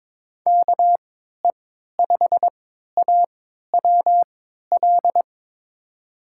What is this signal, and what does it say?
Morse code, heard as a keyed tone, about 22 wpm, spaced as for 13 wpm, 700 Hz. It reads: KE5AWL
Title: USA Call Signs - 6 Characters - 3x Character Spacing - 22wpm